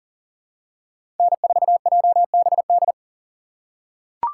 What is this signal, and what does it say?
Morse code, heard as a keyed tone, about 40 wpm, 700 Hz. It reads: N4JBD
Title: USA Call Signs - 3x Character Spacing - 40wpm